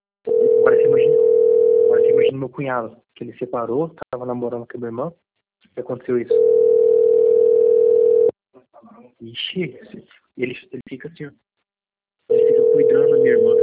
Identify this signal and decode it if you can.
Ringback tone (North America), 440 + 480 Hz, with speech.
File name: rg-1600-unknown- - - .36292